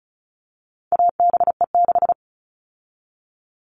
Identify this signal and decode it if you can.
Morse code, heard as a keyed tone, about 34 wpm, 700 Hz. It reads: ABE6
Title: All Cut Numbers - 4-Digit 35wpm